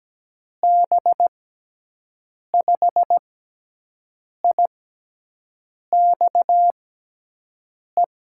Morse code, heard as a keyed tone, about 17 wpm, 700 Hz. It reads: B 5 I X E
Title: USA Call Signs - 6 Characters - 6x Character Spacing - 17wpm